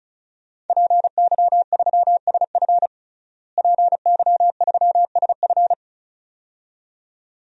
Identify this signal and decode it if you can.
Morse code, heard as a keyed tone, about 36 wpm, 700 Hz. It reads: PY3SF PY3SF
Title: DXCC-100 Call Signs - 5 Characters - Repeated 2x - 35wpm